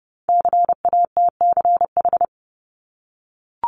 Morse code, heard as a keyed tone, about 30 wpm, 700 Hz. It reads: CATCH
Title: Multi-letter Phonemes 30wpm